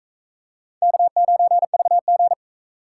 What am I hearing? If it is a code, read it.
Morse code, heard as a keyed tone, about 39 wpm, 700 Hz. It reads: K9VG